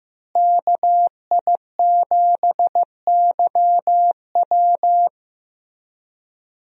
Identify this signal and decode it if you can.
Morse code, heard as a keyed tone, about 15 wpm, 700 Hz. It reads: KI7YW
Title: USA Call Signs - 5 Characters - Repeated 3x - 15wpm